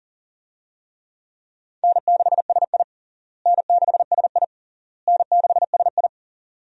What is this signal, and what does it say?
Morse code, heard as a keyed tone, about 40 wpm, 700 Hz. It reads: N6SI N6SI N6SI
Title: USA Call Signs - 4 Characters - Repeated 3x - 40wpm